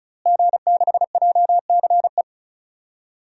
Morse code, heard as a keyed tone, about 36 wpm, 700 Hz. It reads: G6JCE
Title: DXCC-100 Call Signs - 5 Characters - Repeated 2x - 35wpm